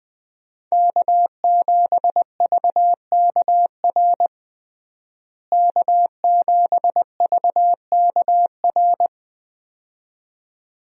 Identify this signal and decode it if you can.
Morse code, heard as a keyed tone, about 21 wpm, 700 Hz. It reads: K7VKR K7VKR
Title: USA Call Signs - 5 Characters - Repeated 2x - 20wpm